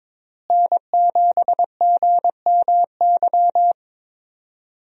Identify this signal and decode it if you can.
Morse code, heard as a keyed tone, about 22 wpm, 700 Hz. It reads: N7GMY
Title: USA Call Signs - 5 Characters - Repeated 3x - 22wpm